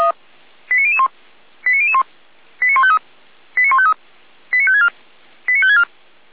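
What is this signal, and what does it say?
Five-tone selective call (ZVEI), heard as quick runs of five tones, tones of about 70 ms. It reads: 89001 89001 89153 89153 89564 89564
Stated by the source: ZVEI five-tone selective call 89020 received by SDR.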